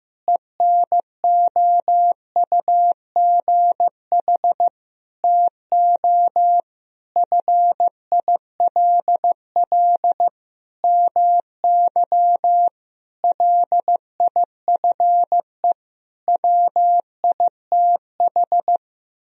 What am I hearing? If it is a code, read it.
Morse code, heard as a keyed tone, about 15 wpm, 700 Hz. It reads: ENOUGH TO FILL MY LIFE WITH
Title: Sentences from Top 1000 Words - Review Part 3 - 15wpm